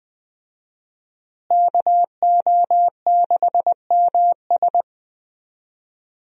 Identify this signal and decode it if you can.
Morse code, heard as a keyed tone, about 21 wpm, 700 Hz. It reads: KO6MS
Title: USA Call Signs - 5 Characters - Repeated 2x - 20wpm